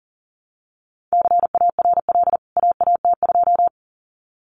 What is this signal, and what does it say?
Morse code, heard as a keyed tone, about 40 wpm, 700 Hz. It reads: CARL AAT2
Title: CWT Exchange 40wpm